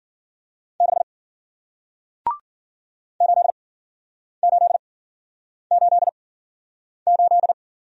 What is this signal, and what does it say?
Morse code, heard as a keyed tone, about 54 wpm, spaced as for 9 wpm, 700 Hz. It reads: 6 8888